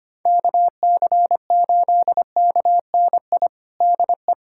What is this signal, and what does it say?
Morse code, heard as a keyed tone, about 25 wpm, 700 Hz. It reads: KC8KNI DE
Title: USA Callsign DE Callsign 25wpm